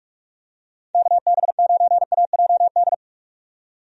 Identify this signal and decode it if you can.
Morse code, heard as a keyed tone, about 47 wpm, 700 Hz. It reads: KB9AJD
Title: USA Call Signs - 6 Characters - 45wpm